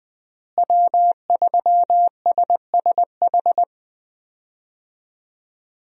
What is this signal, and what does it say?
Morse code, heard as a keyed tone, about 20 wpm, 700 Hz. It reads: W3SSH
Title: USA Call Signs - 5 Characters - Repeated 2x - 20wpm